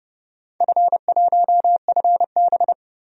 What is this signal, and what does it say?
Morse code, heard as a keyed tone, about 31 wpm, 700 Hz. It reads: F1FB
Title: DXCC-100 Call Signs - 4 Characters - 30wpm